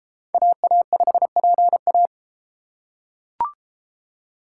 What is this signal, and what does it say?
Morse code, heard as a keyed tone, about 30 wpm, 700 Hz. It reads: AA5PA